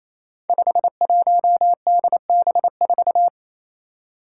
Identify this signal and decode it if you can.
Morse code, heard as a keyed tone, about 27 wpm, 700 Hz. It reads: 51DB4
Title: Five Challenging Letter-Number Combinations 28wpm